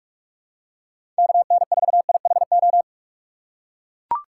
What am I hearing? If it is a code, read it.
Morse code, heard as a keyed tone, about 43 wpm, 700 Hz. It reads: KN4IHO
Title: USA Call Signs - 6x Character Spacing - 45wpm